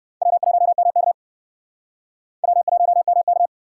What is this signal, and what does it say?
Morse code, heard as a keyed tone, about 65 wpm, 700 Hz. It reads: W1MK W1MK